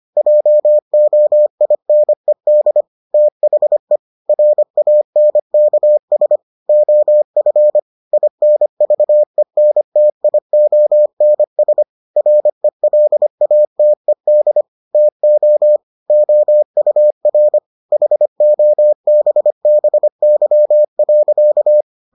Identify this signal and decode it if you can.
Morse code, heard as a keyed tone, about 25 wpm, 600 Hz. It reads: JOINED THE RANKS OF INVENTIONS RELATED TO OUR HOBBY.